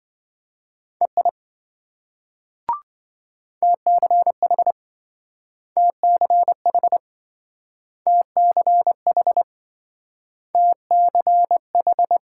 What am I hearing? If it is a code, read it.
Morse code, heard as a keyed tone, about 26 wpm, 700 Hz. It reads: EI TCH TCH TCH TCH